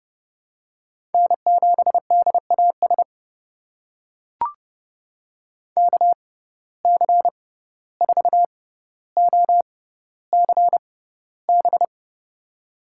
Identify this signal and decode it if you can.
Morse code, heard as a keyed tone, about 30 wpm, spaced as for 15 wpm, 700 Hz. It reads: N7DAS KC4OCB
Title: USA Call Signs - 6x Character Spacing - 30wpm